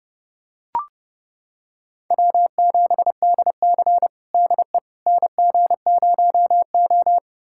Morse code, heard as a keyed tone, about 30 wpm, 700 Hz. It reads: W7DC DE NG0O
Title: USA Callsign DE Callsign - 4 Characters - 30wpm